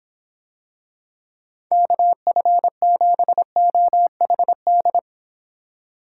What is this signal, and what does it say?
Morse code, heard as a keyed tone, about 25 wpm, 700 Hz. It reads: KF7OHD